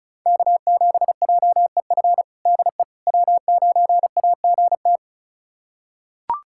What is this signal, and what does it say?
Morse code, heard as a keyed tone, about 36 wpm, 700 Hz. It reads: K7JEF DE W9AGT
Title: USA Callsign DE Callsign - 5 Characters - 35wpm